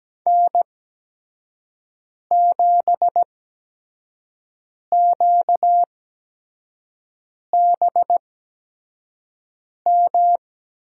Morse code, heard as a keyed tone, about 17 wpm, spaced as for 5 wpm, 700 Hz. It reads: N7QBM
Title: USA Call Signs - 5 Characters - 8x Character Spacing - 17wpm